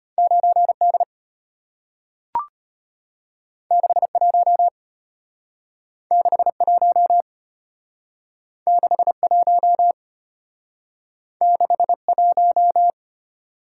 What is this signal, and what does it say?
Morse code, heard as a keyed tone, about 35 wpm, 690 Hz. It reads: 9D 61 61 61 61